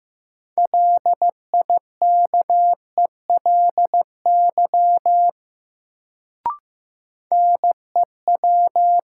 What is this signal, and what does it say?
Morse code, heard as a keyed tone, about 15 wpm, 700 Hz. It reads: LIKELY NEW
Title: Sets of 2 Words - New Word Drill - 15wpm